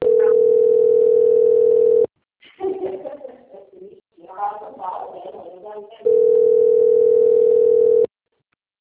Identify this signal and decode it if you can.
Ringback tone (North America), 440 + 480 Hz, with speech.
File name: rg-1600-unknown- - - .200